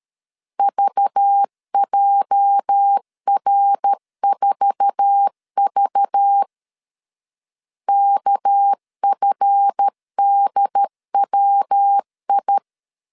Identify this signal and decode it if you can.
Morse code, heard as a keyed tone, about 13 wpm, 800 Hz. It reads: VJR4V KFDWI
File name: LEZIONI / 8 YJ45 / 8 YJ4 12wpm 2151